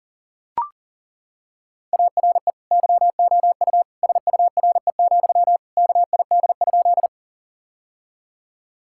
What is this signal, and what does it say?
Morse code, heard as a keyed tone, about 37 wpm, 700 Hz. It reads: ARE YOU SURE, KID?
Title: Sentences from Top 1600 Words 40wpm